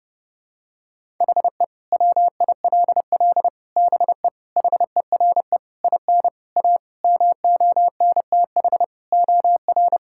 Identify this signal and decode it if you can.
Morse code, heard as a keyed tone, about 30 wpm, 700 Hz. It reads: HE WILL BE HERE IN A MONTH OR
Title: Sentences from Top 1000 Words - Review Part 5 - 30wpm